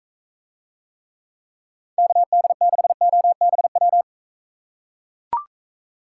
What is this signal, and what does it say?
Morse code, heard as a keyed tone, about 40 wpm, 700 Hz. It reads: KD6OBW